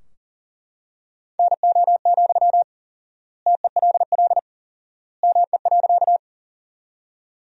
Morse code, heard as a keyed tone, about 40 wpm, 700 Hz. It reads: NO, TELL ME.
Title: 4x Word-Spacing - Sentences from Top 200 Words - 40wpm